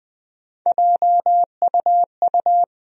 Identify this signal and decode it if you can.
Morse code, heard as a keyed tone, about 20 wpm, 700 Hz. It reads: JUU
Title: USA Call Sign Postfixes - 3 Letters 20wpm